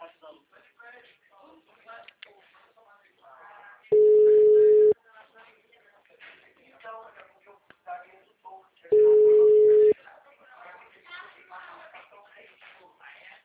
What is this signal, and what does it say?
Ringback tone (Europe, 425 Hz), with speech.
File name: out- -2024- - - .69690